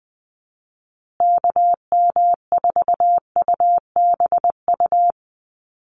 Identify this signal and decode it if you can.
Morse code, heard as a keyed tone, about 20 wpm, 700 Hz. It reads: KM4UBU
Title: USA Call Signs Repeated 3x 20wpm